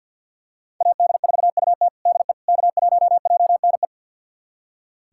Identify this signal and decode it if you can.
Morse code, heard as a keyed tone, about 46 wpm, 700 Hz. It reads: AD4UT DE K1JNE
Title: USA Callsign DE Callsign - 5 Characters - 50wpm